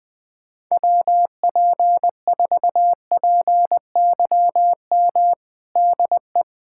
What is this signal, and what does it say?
Morse code, heard as a keyed tone, about 21 wpm, 700 Hz. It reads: WP4PYM DE
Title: USA Callsign DE Callsign - 6 Characters - 20wpm